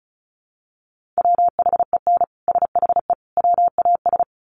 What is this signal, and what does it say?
Morse code, heard as a keyed tone, about 34 wpm, 700 Hz. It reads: WHEN SHE WAS
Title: Sets of 3 Words 35wpm